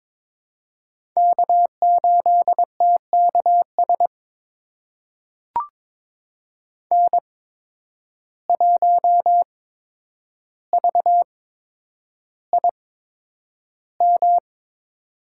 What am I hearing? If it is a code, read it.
Morse code, heard as a keyed tone, about 22 wpm, spaced as for 9 wpm, 700 Hz. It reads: K8TKS N1VIM
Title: USA Call Signs - 5 Characters - 8x Character Spacing - 22wpm